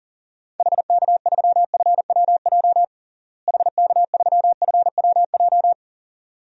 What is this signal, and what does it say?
Morse code, heard as a keyed tone, about 40 wpm, 700 Hz. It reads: HK3FWJ HK3FWJ
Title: DXCC-100 Call Signs - 6 Characters - Repeated 2x - 40wpm